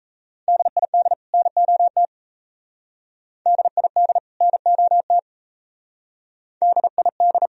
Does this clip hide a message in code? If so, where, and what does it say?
Yes: Morse code from 0.5 to 7.5 s, heard as a keyed tone, about 35 wpm, 700 Hz. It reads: DID NOT DID NOT DID